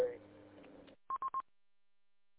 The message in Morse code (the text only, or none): none